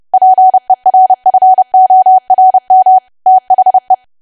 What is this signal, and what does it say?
Morse code, heard as a keyed tone, about 30 wpm, 750 Hz. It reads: PERFORM THE
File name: cwa / W1AW-Level3 / 30WPM